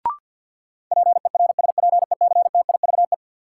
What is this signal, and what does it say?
Morse code, heard as a keyed tone, about 53 wpm, 700 Hz. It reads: PERSPECTIVE
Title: Sentences from Top 1200 Words - New Word Drill - 50wpm